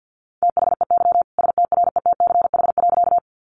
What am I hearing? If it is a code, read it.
Morse code, heard as a keyed tone, about 50 wpm, 700 Hz. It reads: THEY STRETCH.
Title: Easy Phrases 50wpm